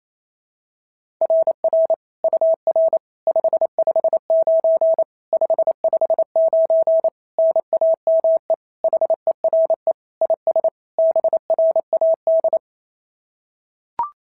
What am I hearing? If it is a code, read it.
Morse code, heard as a keyed tone, about 27 wpm, 690 Hz. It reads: RR UR 559 559 NAME HERE IS BRAD
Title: SOTA QSO Practice 28wpm